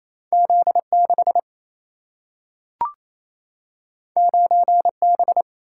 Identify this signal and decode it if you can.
Morse code, heard as a keyed tone, about 28 wpm, 700 Hz. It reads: Z 6 9 B
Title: Single-Letter Single-Number 28wpm